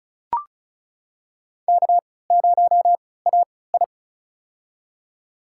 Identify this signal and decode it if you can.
Morse code, heard as a keyed tone, about 34 wpm, spaced as for 20 wpm, 700 Hz. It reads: K0AI
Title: USA Call Signs - 3x Character Spacing - 35wpm